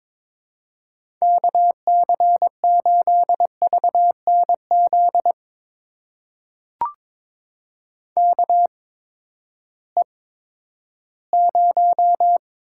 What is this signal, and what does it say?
Morse code, heard as a keyed tone, about 21 wpm, spaced as for 12 wpm, 700 Hz. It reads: KC8VNZ KE0
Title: USA Call Signs - 6 Characters - 8x Character Spacing - 22wpm